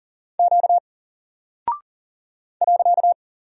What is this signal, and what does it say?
Morse code, heard as a keyed tone, about 41 wpm, 700 Hz. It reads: Q .
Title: Single Character (Letter:Number:Punctuation:Prosign) 40wpm